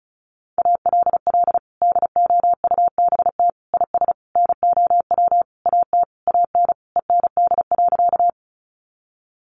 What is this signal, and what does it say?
Morse code, heard as a keyed tone, about 34 wpm, 700 Hz. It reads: ALL DOUBT IS NOW AT AN END.
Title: Sentences from Top 1500 Words 35wpm